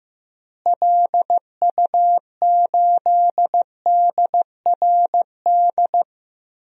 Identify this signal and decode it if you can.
Morse code, heard as a keyed tone, about 15 wpm, 700 Hz. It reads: LU8DRD
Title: DXCC-100 Call Signs - 6 Characters - Repeated 3x - 15wpm